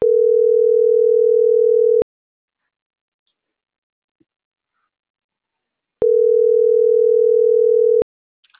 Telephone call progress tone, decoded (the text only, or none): ringback tone